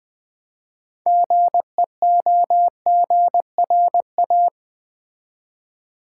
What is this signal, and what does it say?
Morse code, heard as a keyed tone, about 20 wpm, 700 Hz. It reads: GEOGRA
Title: Build A Word - 3 Letters At A Time - 9-Letter Words - 20wpm